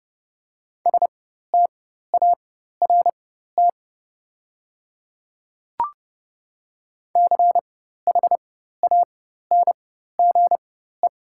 Morse code, heard as a keyed tone, about 30 wpm, spaced as for 14 wpm, 700 Hz. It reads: START CHANGE
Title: 4x Character Spacing - Top 300 Words - 30wpm